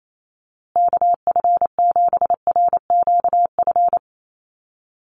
Morse code, heard as a keyed tone, about 28 wpm, 700 Hz. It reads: KF7RQF
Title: USA Call Signs Repeated 2x 28wpm